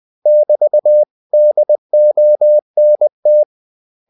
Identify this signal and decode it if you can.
Morse code, heard as a keyed tone, about 20 wpm, 600 Hz. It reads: <BT>DONT
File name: CW / Practice / 500MostCommonEnglishWords20